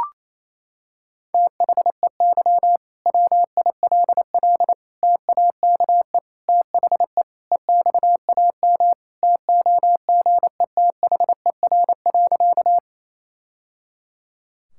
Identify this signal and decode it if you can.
Morse code, heard as a keyed tone, about 27 wpm, 700 Hz. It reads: THEY WILL TAKE THE EXAM TOGETHER.
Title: Sentences from Top 1000 Words - Review Part 5 - 28wpm